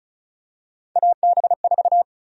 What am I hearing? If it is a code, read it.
Morse code, heard as a keyed tone, about 36 wpm, 700 Hz. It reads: AB4
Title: USA Call Sign Prefixes - 35wpm